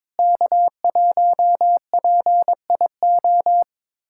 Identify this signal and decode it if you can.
Morse code, heard as a keyed tone, about 22 wpm, 700 Hz. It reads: K1PIO
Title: USA Callsign DE Callsign 22wpm